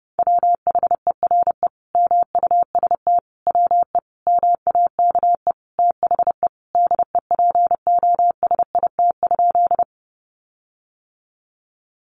Morse code, heard as a keyed tone, about 28 wpm, 700 Hz. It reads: WHERE MUST WE MAKE THE DEPOSIT?
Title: Sentences from Top 1300 Words 30wpm